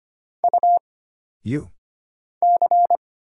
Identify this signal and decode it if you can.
Morse code, heard as a keyed tone, about 25 wpm, 700 Hz. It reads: U C
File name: Single Character (Letter:Number:Punctuation:Prosign) - Mind-Melt - 25wpm